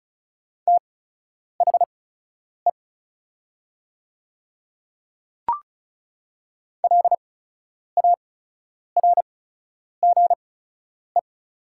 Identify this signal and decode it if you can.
Morse code, heard as a keyed tone, about 36 wpm, spaced as for 10 wpm, 700 Hz. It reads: THE LARGE
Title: 8x Character Spacing - Top 300 Words - 35wpm